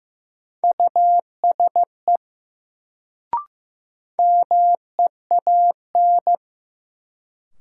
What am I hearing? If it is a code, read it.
Morse code, heard as a keyed tone, about 15 wpm, 700 Hz. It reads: USE MEAN
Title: Sets of 2 Words - New Word Drill - 15wpm